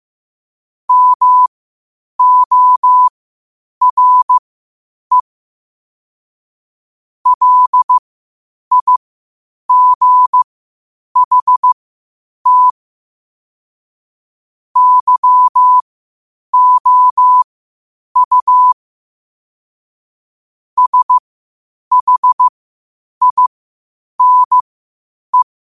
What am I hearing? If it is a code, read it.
Morse code, heard as a keyed tone, about 15 wpm, spaced as for 8 wpm, 1000 Hz. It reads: MORE LIGHT YOU SHINE
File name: qotdfiles / 8wpm